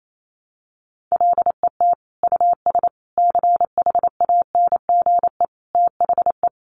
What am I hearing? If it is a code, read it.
Morse code, heard as a keyed tone, about 28 wpm, 700 Hz. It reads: LET US CHANGE THE
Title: Sentences from Top 800 Words 28wpm